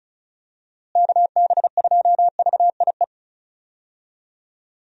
Morse code, heard as a keyed tone, about 36 wpm, 700 Hz. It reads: KB2VIE
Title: USA Call Signs - 6 Characters - Repeated 2x - 35wpm